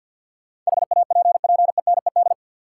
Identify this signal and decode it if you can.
Morse code, heard as a keyed tone, about 52 wpm, 700 Hz. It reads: HAPPENED
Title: Sentences from Top 1000 Words - Review Part 2 - New Word Drill - 50wpm